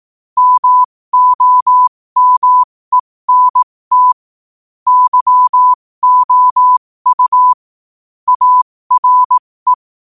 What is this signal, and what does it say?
Morse code, heard as a keyed tone, about 18 wpm, spaced as for 15 wpm, 1000 Hz. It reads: MOMENT YOU ARE
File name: qotdfiles / 15wpm